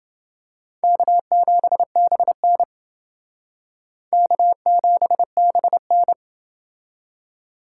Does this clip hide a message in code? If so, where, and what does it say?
Yes: Morse code from 0.8 to 6.1 s, heard as a keyed tone, about 29 wpm, 700 Hz. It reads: K7BN K7BN